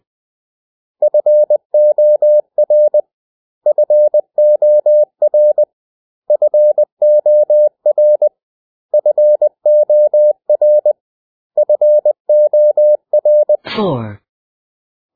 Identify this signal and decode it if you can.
Morse code, heard as a keyed tone, about 20 wpm, 600 Hz. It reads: FOR FOR FOR FOR FOR
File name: cw / 100 Common English Words